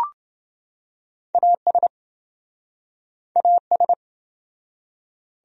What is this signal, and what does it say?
Morse code, heard as a keyed tone, about 30 wpm, 700 Hz. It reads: AS AS